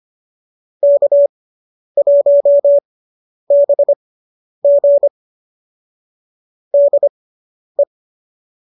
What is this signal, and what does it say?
Morse code, heard as a keyed tone, about 25 wpm, spaced as for 10 wpm, 570 Hz. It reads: K1BG DE